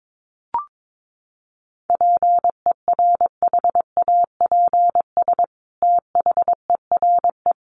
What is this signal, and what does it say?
Morse code, heard as a keyed tone, about 22 wpm, 700 Hz. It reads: PERHAPS THERE
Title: Sentences from Top 700 Words 22wpm